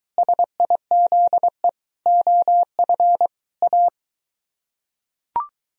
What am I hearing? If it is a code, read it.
Morse code, heard as a keyed tone, about 22 wpm, 700 Hz. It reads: SIZE OF A